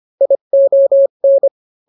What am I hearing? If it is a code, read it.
Morse code, heard as a keyed tone, about 25 wpm, 550 Hz. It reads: ION